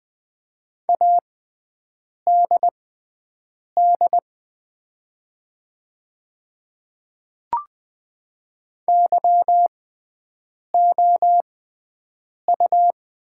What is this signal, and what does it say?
Morse code, heard as a keyed tone, about 20 wpm, spaced as for 7 wpm, 700 Hz. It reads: ADD YOU
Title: 6x Character Spacing - Top 200 Words - 20wpm